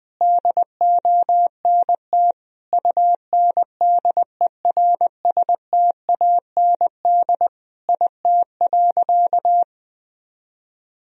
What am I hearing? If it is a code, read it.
Morse code, heard as a keyed tone, about 21 wpm, 700 Hz. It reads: DONT UNDERSTAND IT.
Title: Sentences from Top 1000 Words - Review Part 4 - 20wpm